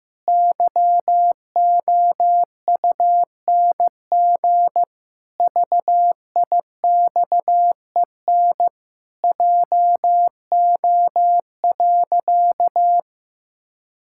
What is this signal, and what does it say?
Morse code, heard as a keyed tone, about 15 wpm, 700 Hz. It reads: YOUNG VIXEN JO.
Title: Pangram Challenge v2 - 15wpm